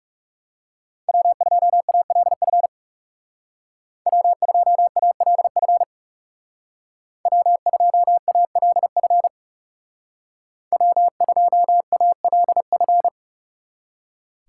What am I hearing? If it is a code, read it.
Morse code, heard as a keyed tone, about 37 wpm, 700 Hz. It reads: W2ALF W2ALF W2ALF W2ALF
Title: Speed Racing - USA 1x3 Call Signs 30wpm